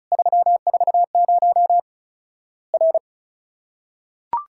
Morse code, heard as a keyed tone, about 34 wpm, 700 Hz. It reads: 340 R
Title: Worked All Europe DX Contest - QTC Exchange - 35wpm